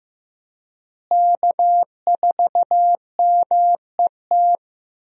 Morse code, heard as a keyed tone, about 15 wpm, 700 Hz. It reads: K4MET
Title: USA Call Signs - 5 Characters - Repeated 2x - 15wpm